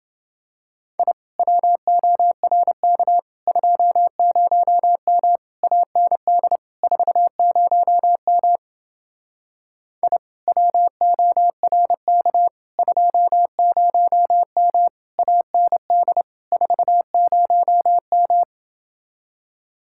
Morse code, heard as a keyed tone, about 29 wpm, 700 Hz. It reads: I WORK 20M AND 40M I WORK 20M AND 40M